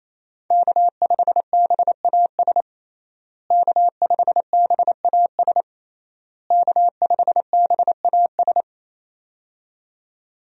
Morse code, heard as a keyed tone, about 27 wpm, 700 Hz. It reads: K5BAS K5BAS K5BAS
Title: USA Call Signs - 5 Characters - Repeated 3x - 28wpm